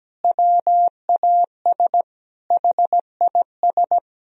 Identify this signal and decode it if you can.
Morse code, heard as a keyed tone, about 17 wpm, 700 Hz. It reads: WAS HIS
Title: Sentences from Top 1000 Words - Review Part 4 - 17wpm